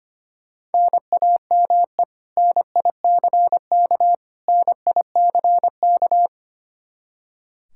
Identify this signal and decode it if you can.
Morse code, heard as a keyed tone, about 25 wpm, 700 Hz. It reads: NAME NICK NICK
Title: Practice QSO - Exchange 1 - Beginning - 25wpm